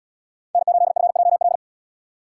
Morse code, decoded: N8FJL

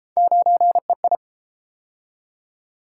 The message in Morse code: 9EI

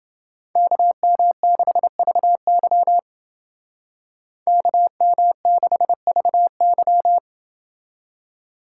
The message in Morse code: KM6VY KM6VY

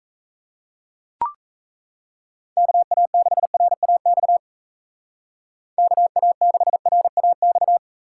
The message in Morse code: KA6RAX KA6RAX

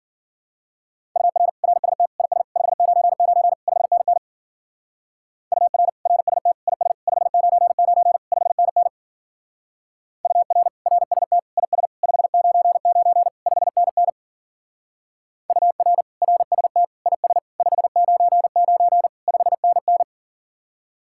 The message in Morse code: UR RST IS 599 5NN UR RST IS 599 5NN UR RST IS 599 5NN UR RST IS 599 5NN

